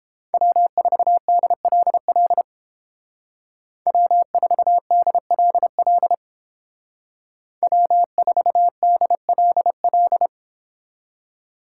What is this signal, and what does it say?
Morse code, heard as a keyed tone, about 30 wpm, 700 Hz. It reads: W4DLL W4DLL W4DLL